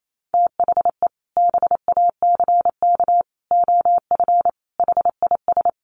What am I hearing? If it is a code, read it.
Morse code, heard as a keyed tone, about 28 wpm, 700 Hz. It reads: THE BACK OF HIS
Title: Sets of 5 Words Encore 28wpm